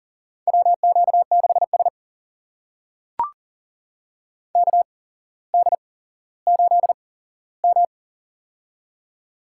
Morse code, heard as a keyed tone, about 40 wpm, spaced as for 17 wpm, 700 Hz. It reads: WQ6S KD8M
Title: USA Call Signs - 4 Characters - 8x Character Spacing - 40wpm